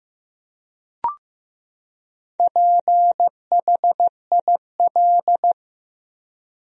Morse code, heard as a keyed tone, about 15 wpm, 700 Hz. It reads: PHIL
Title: USA Common Names - 15wpm